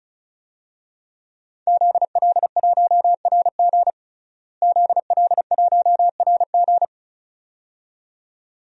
Morse code, heard as a keyed tone, about 36 wpm, 700 Hz. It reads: ZL1RG ZL1RG
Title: DXCC-100 Call Signs - 5 Characters - Repeated 2x - 35wpm